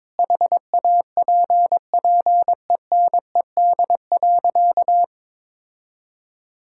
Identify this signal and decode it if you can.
Morse code, heard as a keyed tone, about 22 wpm, 700 Hz. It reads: HAPPENED.
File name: practice / 22wpm / Sentences with 8-Word Phrases - 22wpm